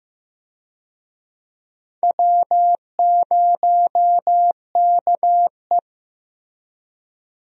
Morse code, heard as a keyed tone, about 15 wpm, 700 Hz. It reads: W0KE
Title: USA Call Signs - 4 Characters - Repeated 2x - 15wpm